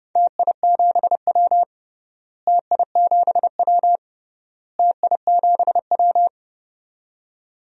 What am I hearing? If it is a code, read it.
Morse code, heard as a keyed tone, about 30 wpm, 700 Hz. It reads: TI7W TI7W TI7W
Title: DXCC-100 Call Signs - 4 Characters - Repeated 3x - 30wpm